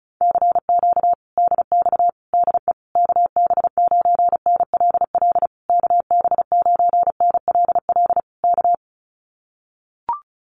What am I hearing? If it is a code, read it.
Morse code, heard as a keyed tone, about 34 wpm, 700 Hz. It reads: CQ DX DE KB9NLL KB9NLL K
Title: Common QSO Phrases Encore 35wpm